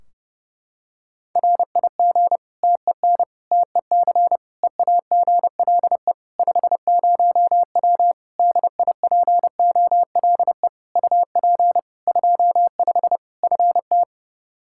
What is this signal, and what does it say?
Morse code, heard as a keyed tone, about 30 wpm, 700 Hz. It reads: RIG TEN TEC EAGLE 50W DIPOLE UP 25 FT
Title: Practice QSO - Exchange 2 - Info - 30wpm